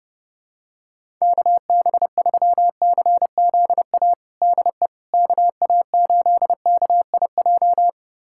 Morse code, heard as a keyed tone, about 30 wpm, 700 Hz. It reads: KB3CZA DE KA8KIJ
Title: USA Callsign DE Callsign - 6 Characters - 30wpm